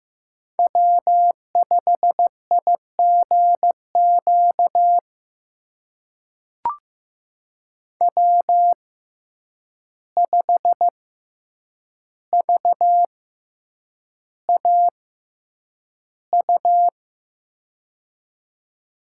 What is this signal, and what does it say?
Morse code, heard as a keyed tone, about 15 wpm, spaced as for 8 wpm, 700 Hz. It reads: W5IGQ W5VAU
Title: USA Call Signs - 5 Characters - 6x Character Spacing - 15wpm